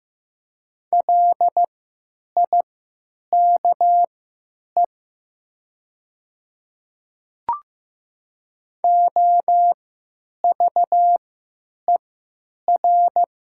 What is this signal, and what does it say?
Morse code, heard as a keyed tone, about 15 wpm, spaced as for 9 wpm, 700 Hz. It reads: LIKE OVER
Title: 3x Character Spacing - Top 100 Words - 15wpm